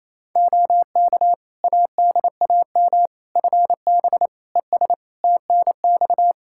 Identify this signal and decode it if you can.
Morse code, heard as a keyed tone, about 28 wpm, 700 Hz. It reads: OK ADAM FB ES TNX
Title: Practice QSO - Exchange 2 - Info - 28wpm